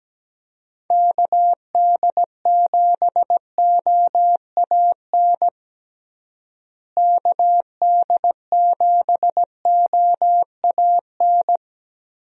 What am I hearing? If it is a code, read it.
Morse code, heard as a keyed tone, about 17 wpm, 700 Hz. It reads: KD7OAN KD7OAN
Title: USA Call Signs - 6 Characters - Repeated 3x - 17wpm